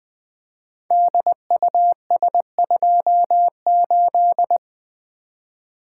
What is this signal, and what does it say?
Morse code, heard as a keyed tone, about 21 wpm, 700 Hz. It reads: DUS28
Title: Code Groups - 5 Letter:Numbers - 20wpm